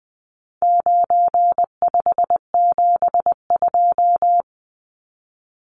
Morse code, heard as a keyed tone, about 20 wpm, 700 Hz. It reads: 9572
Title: Numbers - 4-Digit 20wpm